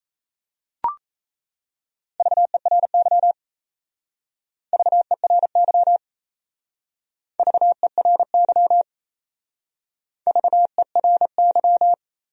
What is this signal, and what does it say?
Morse code, heard as a keyed tone, about 35 wpm, 700 Hz. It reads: VERY VERY VERY VERY